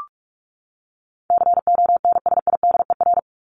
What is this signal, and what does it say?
Morse code, heard as a keyed tone, about 44 wpm, 700 Hz. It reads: CONSIDER
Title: Top 501-750 Words 45wpm